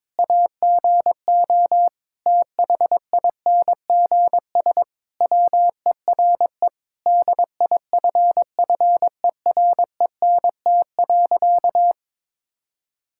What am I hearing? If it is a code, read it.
Morse code, heard as a keyed tone, about 22 wpm, 700 Hz. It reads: AGO THINGS WERE DIFFERENT.
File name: practice / 22wpm / Sentences with 4-Word Phrases - 22wpm